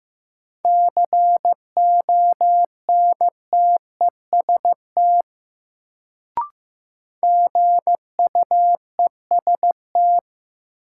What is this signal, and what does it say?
Morse code, heard as a keyed tone, about 15 wpm, 700 Hz. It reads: CONTEST GUEST
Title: Sentences from Top 1600 Words - New Word Drill Alt - 15wpm